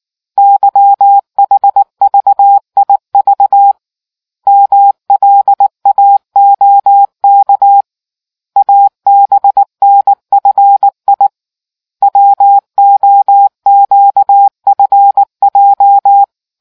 Morse code, heard as a keyed tone, about 19 wpm, 800 Hz. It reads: YHVIV MLAOK ABNFI WOQFJ